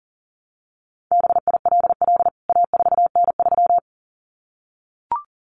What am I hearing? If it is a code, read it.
Morse code, heard as a keyed tone, about 40 wpm, 700 Hz. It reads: BILL A4N3
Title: CWT Exchange 40wpm